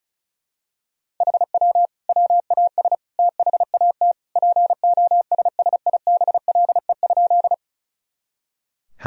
Morse code, heard as a keyed tone, about 36 wpm, 700 Hz. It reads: HW WAS THAT POSSIBLE?
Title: Sentences from Top 500 Words and Abbreviations 35wpm